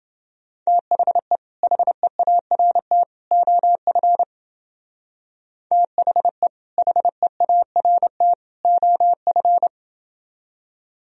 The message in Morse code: THE HEART OF THE HEART OF

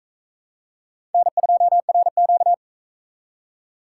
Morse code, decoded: N2RQ

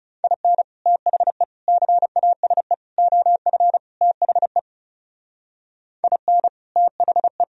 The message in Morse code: IN THE CASE OF THE IN THE

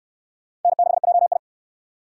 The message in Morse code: N6JI

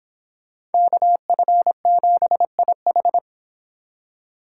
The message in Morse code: KF7IH

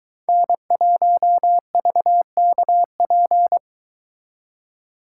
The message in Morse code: N1VKP

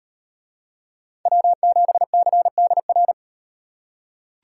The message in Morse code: W7CDR

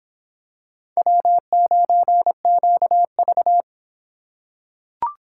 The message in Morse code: W9QV

